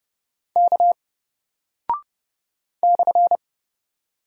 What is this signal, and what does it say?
Morse code, heard as a keyed tone, about 30 wpm, 700 Hz. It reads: K /
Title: Single Character (Letter:Number:Punctuation:Prosign) 30wpm